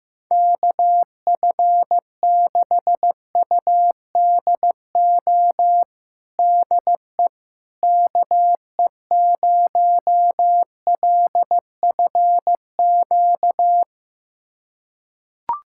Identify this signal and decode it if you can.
Morse code, heard as a keyed tone, about 15 wpm, 700 Hz. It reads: KF6UDO DE KE0LFQ
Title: USA Callsign DE Callsign 15wpm